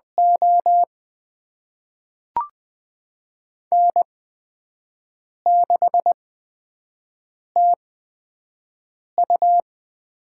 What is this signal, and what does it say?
Morse code, heard as a keyed tone, about 20 wpm, spaced as for 5 wpm, 700 Hz. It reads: O N6TU
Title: USA Call Signs - 4 Characters - 8x Character Spacing - 20wpm